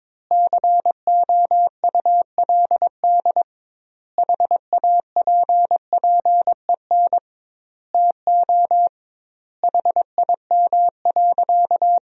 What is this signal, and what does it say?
Morse code, heard as a keyed tone, about 22 wpm, 700 Hz. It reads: COULD HAPPEN TO HIM.
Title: 2x Word-Spacing - Sentences from Top 400 Words - 22wpm